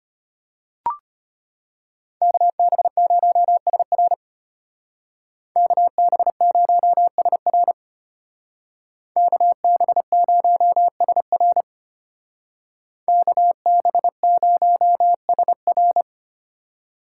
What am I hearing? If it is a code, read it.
Morse code, heard as a keyed tone, about 33 wpm, 700 Hz. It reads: KB0SR KB0SR KB0SR KB0SR